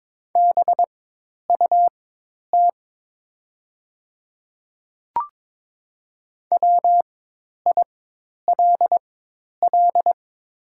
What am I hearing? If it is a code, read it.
Morse code, heard as a keyed tone, about 22 wpm, spaced as for 10 wpm, 700 Hz. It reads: BUT WILL
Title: 4x Character Spacing - Top 100 Words - 22wpm